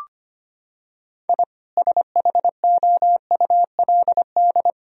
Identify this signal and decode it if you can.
Morse code, heard as a keyed tone, about 25 wpm, 700 Hz. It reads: I SHOULD
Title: Sentences from Top 1000 Words - Review Part 3 - 25wpm